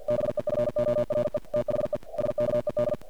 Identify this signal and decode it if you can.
Morse code, heard as a keyed tone, about 48 wpm, 590 Hz, with noise. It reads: BEFORE THE SKED
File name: archived / CRSnet